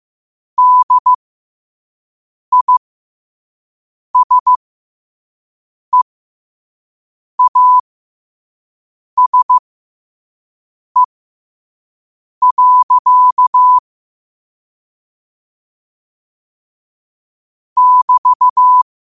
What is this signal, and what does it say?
Morse code, heard as a keyed tone, about 15 wpm, spaced as for 5 wpm, 1000 Hz. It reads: DISEASE. <BT>
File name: qotdfiles / 5wpm